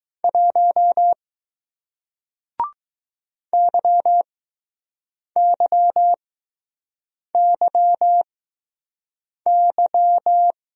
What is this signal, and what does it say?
Morse code, heard as a keyed tone, about 20 wpm, spaced as for 6 wpm, 700 Hz. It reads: 1 YYYY